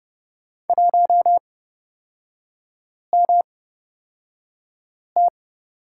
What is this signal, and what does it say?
Morse code, heard as a keyed tone, about 30 wpm, spaced as for 5 wpm, 700 Hz. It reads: 1MT
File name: Lesson 37e - Letter x - Three Characters 30wpm